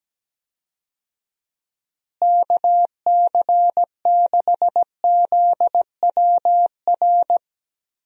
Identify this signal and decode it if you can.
Morse code, heard as a keyed tone, about 17 wpm, 700 Hz. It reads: KC6ZWR
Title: USA Call Signs - 6 Characters - Repeated 3x - 17wpm